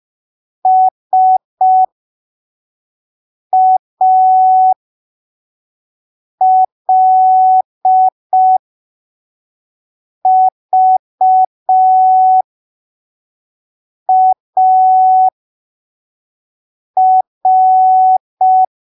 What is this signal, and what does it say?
Morse code, heard as a keyed tone, about 5 wpm, spaced as for 3 wpm, 750 Hz. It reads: SALVAR
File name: arquivos / texto / texto 02 5ppm